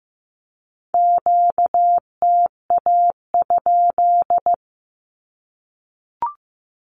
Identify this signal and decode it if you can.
Morse code, heard as a keyed tone, about 15 wpm, 700 Hz. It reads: QTA?
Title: Q-Codes Repeated 3x 15wpm